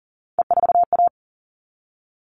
Morse code, heard as a keyed tone, about 40 wpm, 700 Hz. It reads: E4A
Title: All Cut Numbers - 3-Digit 40wpm